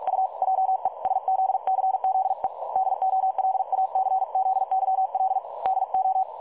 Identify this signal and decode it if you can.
Morse code, heard as a keyed tone, about 46 wpm, 760 Hz. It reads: U JESZCZE DOLACZYL DO
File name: dlazaawansowanych / nagraniazpasma / websdr recording 3565.0kHz